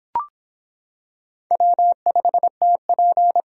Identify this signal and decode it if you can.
Morse code, heard as a keyed tone, about 26 wpm, 700 Hz. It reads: W5TP